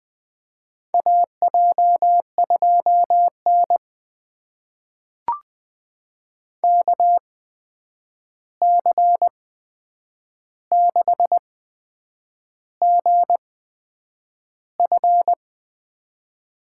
Morse code, heard as a keyed tone, about 20 wpm, spaced as for 8 wpm, 700 Hz. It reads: AJ2N KC6GF
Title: USA Call Signs - 8x Character Spacing - 20wpm